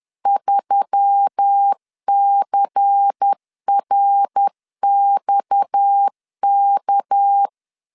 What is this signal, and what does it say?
Morse code, heard as a keyed tone, about 10 wpm, 800 Hz. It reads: 3CRXK
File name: LEZIONI / 6 PKX0 / 6 PKX 10wpm 2426